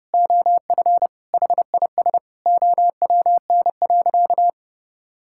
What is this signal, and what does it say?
Morse code, heard as a keyed tone, about 32 wpm, 700 Hz. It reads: OF HIS OWN.
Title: Sentences from Top 1000 Words - Review Part 6 - 30wpm